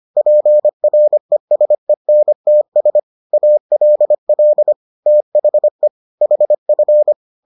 Morse code, heard as a keyed tone, about 25 wpm, 600 Hz. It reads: PRESENTS ALL THE HF